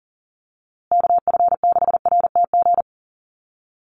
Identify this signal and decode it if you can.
Morse code, heard as a keyed tone, about 40 wpm, 700 Hz. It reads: KF6RTG
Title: USA Call Signs Repeated 3x 40wpm